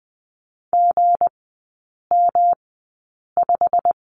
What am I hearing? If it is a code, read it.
Morse code, heard as a keyed tone, about 20 wpm, spaced as for 8 wpm, 690 Hz. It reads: GM5
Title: 159 Lesson 26e - Letter g - Three Characters 20wpm